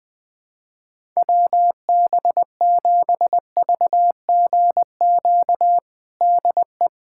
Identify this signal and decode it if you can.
Morse code, heard as a keyed tone, about 20 wpm, 700 Hz. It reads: WB7VGQ DE
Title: USA Callsign DE Callsign - 6 Characters - 20wpm